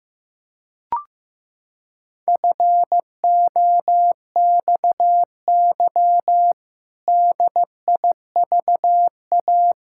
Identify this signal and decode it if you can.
Morse code, heard as a keyed tone, about 15 wpm, 700 Hz. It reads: FOXY DIVA
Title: Pangram Challenge v2 - 15wpm